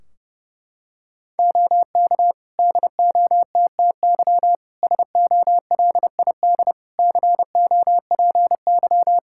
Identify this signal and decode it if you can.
Morse code, heard as a keyed tone, about 31 wpm, 700 Hz. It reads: OK DOTTY SOLID COPY
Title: K1USN SST QSO Practice 30wpm